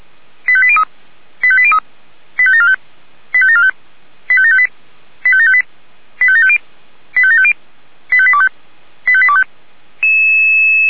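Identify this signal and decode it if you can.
Five-tone selective call (ZVEI), heard as quick runs of five tones, tones of about 70 ms. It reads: 85803 85803 85745 85745 85758 85758 85750 85750 85825 85825
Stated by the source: ZVEI five-tone selective call 85726 received on SDR.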